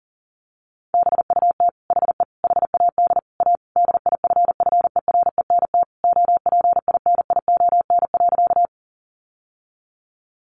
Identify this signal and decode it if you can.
Morse code, heard as a keyed tone, about 40 wpm, 700 Hz. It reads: BUT HE HAD A DIFFERENT OPINION.
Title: Sentences from Top 1200 Words 40wpm